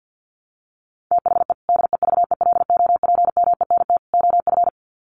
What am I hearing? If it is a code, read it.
Morse code, heard as a keyed tone, about 50 wpm, 700 Hz. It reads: THE DEVELOPMENT OF
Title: Sets of 3 Words 50wpm